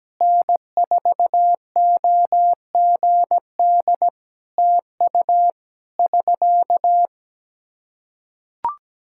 Morse code, heard as a keyed tone, about 17 wpm, 700 Hz. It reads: N4OGD TU <SK>
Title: K1USN SST QSO Practice 17wpm